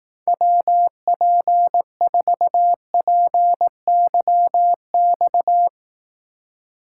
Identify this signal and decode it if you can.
Morse code, heard as a keyed tone, about 18 wpm, 700 Hz. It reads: WP4PYX
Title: Speed Racing - USA 2x3 Call Signs 15wpm